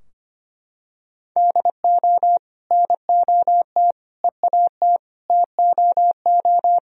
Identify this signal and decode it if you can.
Morse code, heard as a keyed tone, about 25 wpm, 700 Hz. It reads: DO NOT EAT TOO
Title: Sentences from Top 1000 Words - Review Part 2 - 25wpm